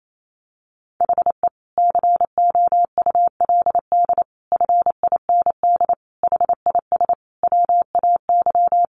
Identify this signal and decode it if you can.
Morse code, heard as a keyed tone, about 28 wpm, 700 Hz. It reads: HE COULD FIND HIS WAY
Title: Sentences from Top 1400 Words 28wpm